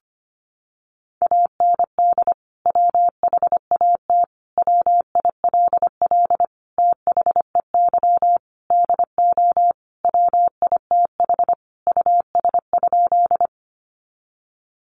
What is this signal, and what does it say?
Morse code, heard as a keyed tone, about 25 wpm, 700 Hz. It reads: AND WHAT WILL THEY DO WITH US?
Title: Sentences from Top 200 Words 25wpm